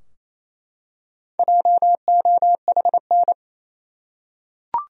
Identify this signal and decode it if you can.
Morse code, heard as a keyed tone, about 28 wpm, 700 Hz. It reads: JOHN
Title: USA Common Names - Max 4 Letters - 28wpm